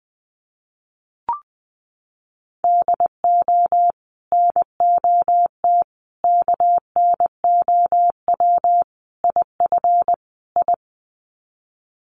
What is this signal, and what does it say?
Morse code, heard as a keyed tone, about 19 wpm, 700 Hz. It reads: DO NOT KNOW IF I
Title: Sets of 5 Words 20wpm